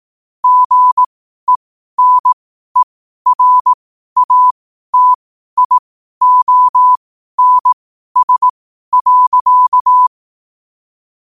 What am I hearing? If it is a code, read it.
Morse code, heard as a keyed tone, about 18 wpm, spaced as for 12 wpm, 1000 Hz. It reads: GENERATIONS.
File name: qotdfiles / 12wpm